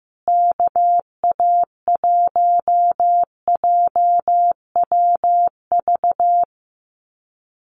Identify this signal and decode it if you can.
Morse code, heard as a keyed tone, about 15 wpm, 700 Hz. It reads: KA1JWV
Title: Common QSO Phrases Encore 15wpm